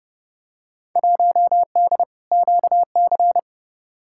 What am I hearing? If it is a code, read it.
Morse code, heard as a keyed tone, about 29 wpm, 700 Hz. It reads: 1D QC
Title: Field Day Exchanges Encore - 30wpm